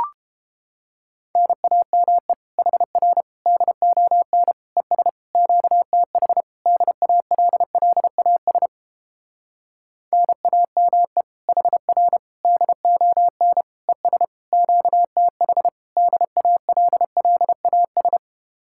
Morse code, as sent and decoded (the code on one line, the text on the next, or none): -. .- -- . / .... .-. / -.. --- -. / . ... / --.- - .... / -.. .- .-.. .-.. .- ... / -. .- -- . / .... .-. / -.. --- -. / . ... / --.- - .... / -.. .- .-.. .-.. .- ...
NAME HR DON ES QTH DALLAS NAME HR DON ES QTH DALLAS